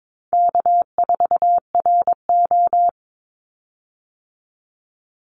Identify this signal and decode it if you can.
Morse code, heard as a keyed tone, about 22 wpm, 700 Hz. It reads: K4RO
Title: Call Signs Repeated 3x - 22wpm